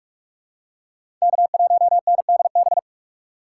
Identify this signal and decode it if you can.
Morse code, heard as a keyed tone, about 46 wpm, 700 Hz. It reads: K1NDB
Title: USA Call Signs - 5 Characters - 6x Character Spacing - 45wpm